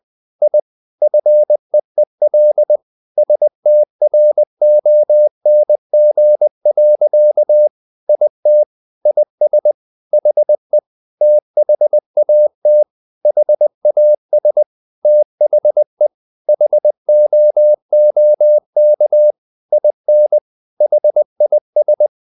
Morse code, heard as a keyed tone, about 20 wpm, 600 Hz. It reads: I FEEL STRONG. IT IS HE THAT HAS THE HOOK IN HIS
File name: the old man and the sea / the old man and the sea part 1 20wpm